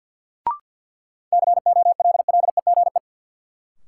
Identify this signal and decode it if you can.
Morse code, heard as a keyed tone, about 46 wpm, 700 Hz. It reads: COLLEGE
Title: Sentences from Top 1300 Words - New Word Drill Alt - 50wpm